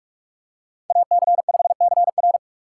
Morse code, heard as a keyed tone, about 43 wpm, 690 Hz. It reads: AC5CR
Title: USA Call Signs - 5 Characters - 2x Character Spacing - 45wpm